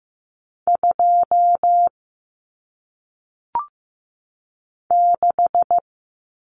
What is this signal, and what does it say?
Morse code, heard as a keyed tone, about 15 wpm, 700 Hz. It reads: 2 6
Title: Numbers - 1-Digit 15wpm